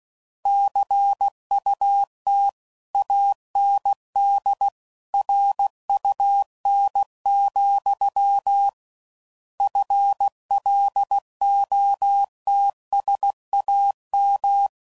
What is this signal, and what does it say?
Morse code, heard as a keyed tone, about 16 wpm, 800 Hz. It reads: CUT AND RUN, FLOTSAM
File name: Mixed / Misc / WordPairs